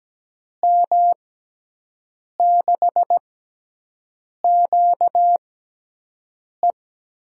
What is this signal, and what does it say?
Morse code, heard as a keyed tone, about 17 wpm, spaced as for 6 wpm, 700 Hz. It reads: M6QE
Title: USA Call Signs - 6 Characters - 6x Character Spacing - 17wpm